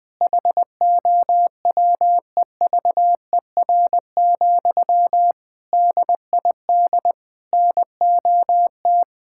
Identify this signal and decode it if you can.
Morse code, heard as a keyed tone, about 20 wpm, 700 Hz. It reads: HOWEVER, DID NOT
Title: Sentences from Top 1000 Words - Review Part 5 - 20wpm